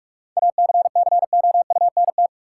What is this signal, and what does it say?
Morse code, heard as a keyed tone, about 47 wpm, 700 Hz. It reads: ACCOUNT
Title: Sentences from Top 900 Words - New Word Drill - 45wpm